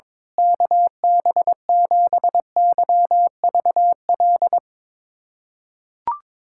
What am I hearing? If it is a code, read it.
Morse code, heard as a keyed tone, about 22 wpm, 700 Hz. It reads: KB7YVL
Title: USA Callsign DE Callsign - 6 Characters - 22wpm